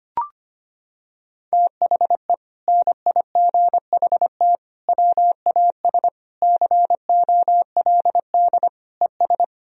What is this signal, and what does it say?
Morse code, heard as a keyed tone, about 25 wpm, 700 Hz. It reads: THE NIGHT WAS COLD ES
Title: Sentences from Top 400 Words and Abbreviations 25wpm